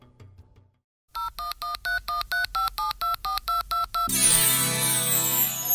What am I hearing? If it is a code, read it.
DTMF, heard as telephone touch tones, tones of about 140 ms. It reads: 0113135724222